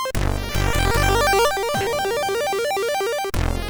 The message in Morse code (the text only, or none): none